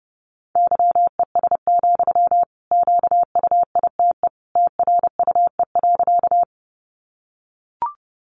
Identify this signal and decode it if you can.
Morse code, heard as a keyed tone, about 30 wpm, 700 Hz. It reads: YES, QUITE TRUE.
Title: Sentences from Top 700 Words 30wpm